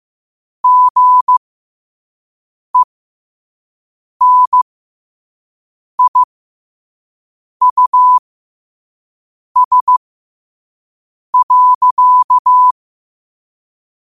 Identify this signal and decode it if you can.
Morse code, heard as a keyed tone, about 15 wpm, spaced as for 5 wpm, 1000 Hz. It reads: GENIUS.
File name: qotdfiles / 5wpm